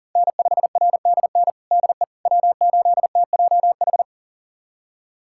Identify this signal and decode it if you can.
Morse code, heard as a keyed tone, about 40 wpm, 700 Hz. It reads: N5RDN DE W8TJH
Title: USA Callsign DE Callsign - 5 Characters - 40wpm